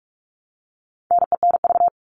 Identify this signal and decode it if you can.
Morse code, heard as a keyed tone, about 44 wpm, 700 Hz. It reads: NENV
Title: All Cut Numbers - 4-Digit 45wpm